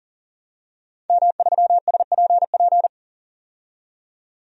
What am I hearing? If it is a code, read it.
Morse code, heard as a keyed tone, about 40 wpm, 700 Hz. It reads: M3SPP
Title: DXCC-100 Call Signs - 5 Characters - 40wpm